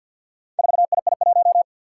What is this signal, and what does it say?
Morse code, heard as a keyed tone, about 52 wpm, 700 Hz. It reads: 4II1
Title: Four Challenging Letter-Number Combinations 50wpm